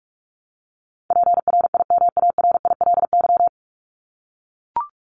Morse code, heard as a keyed tone, about 44 wpm, 700 Hz. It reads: PRIMARILY
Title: Top 1001-1250 Words 45wpm